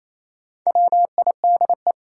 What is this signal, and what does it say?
Morse code, heard as a keyed tone, about 28 wpm, 700 Hz. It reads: WIDE
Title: Sentences from Top 900 Words - New Word Drill - 28wpm